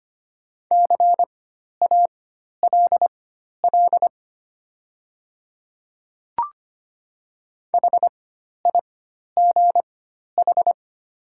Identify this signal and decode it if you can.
Morse code, heard as a keyed tone, about 25 wpm, spaced as for 12 wpm, 690 Hz. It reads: CALL HIGH
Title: 4x Character Spacing - Top 300 Words - 25wpm